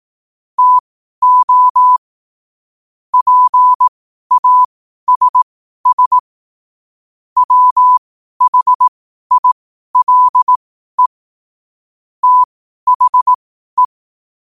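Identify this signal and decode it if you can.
Morse code, heard as a keyed tone, about 18 wpm, spaced as for 12 wpm, 1000 Hz. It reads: TO PASS WHILE THE
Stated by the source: Morse code quotes at 12 words per minute.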